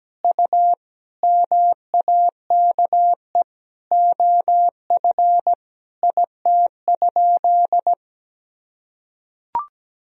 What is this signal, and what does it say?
Morse code, heard as a keyed tone, about 17 wpm, 700 Hz. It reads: U MAKE OF IT?
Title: Sentences from Top 100 Words and Abbreviations 17wpm